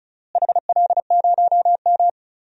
Morse code, heard as a keyed tone, about 34 wpm, 700 Hz. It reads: HL0M
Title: DXCC-100 Call Signs - 4 Characters - Repeated 3x - 35wpm